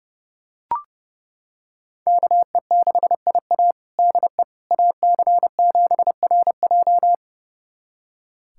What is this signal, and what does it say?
Morse code, heard as a keyed tone, about 31 wpm, 700 Hz. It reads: KE6IA DE AC7RJ
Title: USA Callsign DE Callsign - 5 Characters - 30wpm